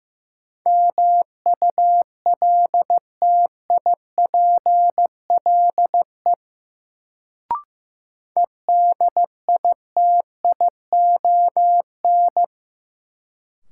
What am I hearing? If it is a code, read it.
Morse code, heard as a keyed tone, about 15 wpm, 700 Hz. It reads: MULTIPLE EDITION
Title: Sets of 5 Words - New Word Drill - 15wpm